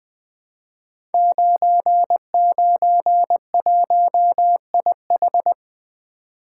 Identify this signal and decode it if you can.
Morse code, heard as a keyed tone, about 20 wpm, 700 Hz. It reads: 991IH
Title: Code Groups - 5 Letter:Numbers - 20wpm